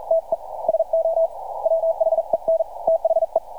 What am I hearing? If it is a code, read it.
Morse code, heard as a keyed tone, about 44 wpm, 650 Hz. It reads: TE SO WHEN THE
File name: archived / CRSnet